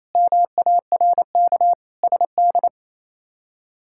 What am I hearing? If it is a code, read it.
Morse code, heard as a keyed tone, about 27 wpm, 700 Hz. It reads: MARK SD
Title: K1USN SST QSO Practice - 28wpm